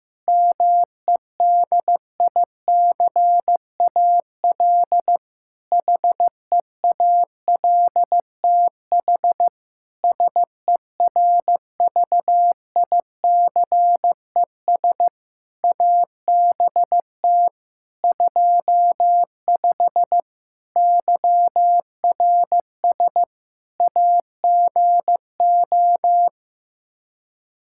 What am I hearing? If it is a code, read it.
Morse code, heard as a keyed tone, about 15 wpm, 700 Hz. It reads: MEDICAL HEALTH SERVICES ABT 25 YRS AGO
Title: Practice QSO - Exchange 3 - More Info - 15wpm